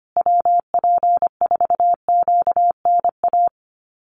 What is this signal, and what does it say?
Morse code, heard as a keyed tone, about 25 wpm, 700 Hz. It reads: WP4QNA
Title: USA Call Signs Repeated 3x 25wpm